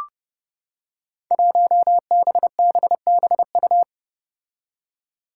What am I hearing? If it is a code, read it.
Morse code, heard as a keyed tone, about 30 wpm, 700 Hz. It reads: 1BBBU
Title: Five Challenging Letter-Number Combinations 30wpm